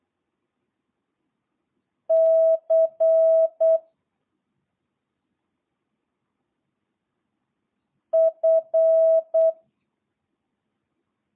Morse code, heard as a keyed tone, about 8 wpm, 650 Hz. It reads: C F